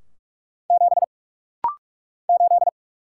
Morse code, heard as a keyed tone, about 48 wpm, 700 Hz. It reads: 7 8
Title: Single Letter-Number - Rapid-Fire 45wpm